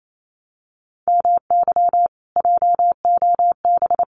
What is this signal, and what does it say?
Morse code, heard as a keyed tone, about 28 wpm, 700 Hz. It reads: MY JOB
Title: Common QSO Phrases Encore 28wpm